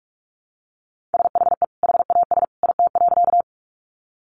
Morse code, heard as a keyed tone, about 47 wpm, 700 Hz. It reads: SHE HAS IT.
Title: Sentences from Top 100 Words 45wpm